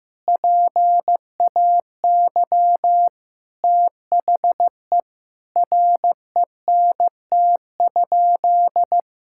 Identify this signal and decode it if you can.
Morse code, heard as a keyed tone, about 15 wpm, 700 Hz. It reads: PAY THE RENT?
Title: Sentences from Top 1000 Words - Review Part 5 - 15wpm